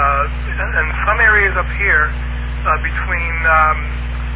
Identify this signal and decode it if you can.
Emergency Alert System (SAME) tones, heard as harsh buzzing bursts, with speech. It is heard from 0.5 to 3.4 s.